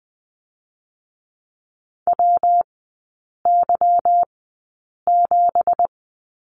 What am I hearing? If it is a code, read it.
Morse code, heard as a keyed tone, about 20 wpm, spaced as for 8 wpm, 690 Hz. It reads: WY7
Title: 205 Lesson 33e - Letter j - Three Characters 20wpm